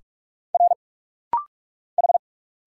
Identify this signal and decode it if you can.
Morse code, heard as a keyed tone, about 46 wpm, 700 Hz. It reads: R H
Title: Single Letters - Rapid-Fire 45wpm